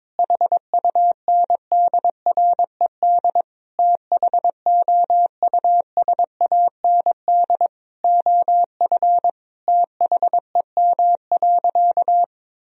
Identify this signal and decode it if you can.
Morse code, heard as a keyed tone, about 22 wpm, 700 Hz. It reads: HUNDRED THOUSAND OF THEM.
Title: Sentences from Top 1000 Words - Review Part 3 - 22wpm